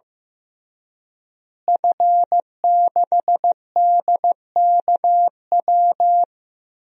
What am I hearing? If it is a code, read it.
Morse code, heard as a keyed tone, about 15 wpm, 700 Hz. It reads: F6DKW
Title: DXCC-100 Call Signs - 5 Characters - Repeated 3x - 15wpm